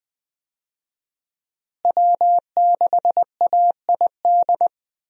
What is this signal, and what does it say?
Morse code, heard as a keyed tone, about 20 wpm, 700 Hz. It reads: W6AID
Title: USA Call Signs - 5 Characters - Repeated 3x - 20wpm